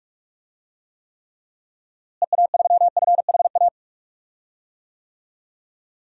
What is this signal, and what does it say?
Morse code, heard as a keyed tone, about 46 wpm, 700 Hz. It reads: EA3FHA
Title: DXCC-100 Call Signs - 6 Characters - Repeated 3x - 45wpm